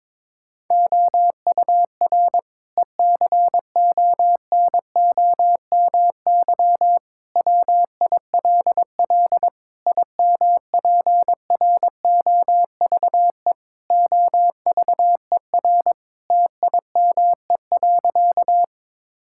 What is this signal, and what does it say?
Morse code, heard as a keyed tone, about 22 wpm, 700 Hz. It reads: OUR ECONOMY WILL IMPROVE OVER TIME.
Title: Sentences from Top 1000 Words - Review Part 5 - 22wpm